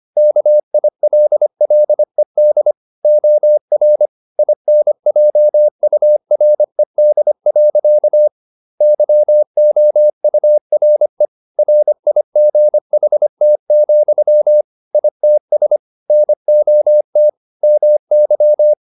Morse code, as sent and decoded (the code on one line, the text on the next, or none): -.- .. .-.. .-.. -.. / --- .-. / .. -. .--- ..- .-. -.. .-.-.- / -.-- --- ..- .-. / .-. .. --. .... - --..-- / .. - ... / -. --- - / -- -.--
KILLED OR INJURED. YOURE RIGHT, ITS NOT MY